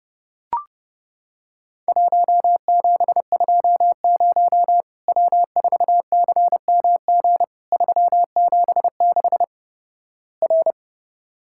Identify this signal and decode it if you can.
Morse code, heard as a keyed tone, about 30 wpm, 700 Hz. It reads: 1720 W4CMG 376 R
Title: Worked All Europe DX Contest - QTC Exchange - 30wpm